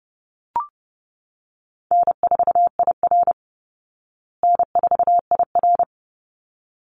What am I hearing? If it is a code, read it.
Morse code, heard as a keyed tone, about 30 wpm, 700 Hz. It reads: N4IR N4IR
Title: Easy Call Signs Repeated 2x - 30wpm